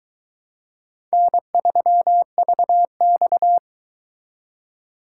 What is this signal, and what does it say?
Morse code, heard as a keyed tone, about 22 wpm, 700 Hz. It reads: N3VX